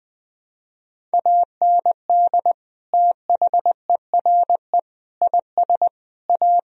Morse code, heard as a keyed tone, about 20 wpm, 700 Hz. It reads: AND THERE IS A
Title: Sentences from Top 1000 Words - Review Part 1 - 20wpm